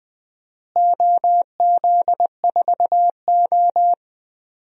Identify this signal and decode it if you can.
Morse code, heard as a keyed tone, about 20 wpm, 700 Hz. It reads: OZ4O
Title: DXCC-100 Call Signs - 4 Characters - Repeated 2x - 20wpm